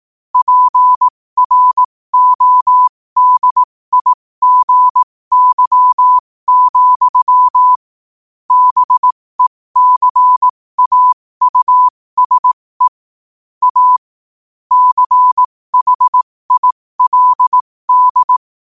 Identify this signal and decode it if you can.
Morse code, heard as a keyed tone, about 18 wpm, spaced as for 15 wpm, 1000 Hz. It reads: PRODIGY, BECAUSE A CHILD
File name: qotdfiles / 15wpm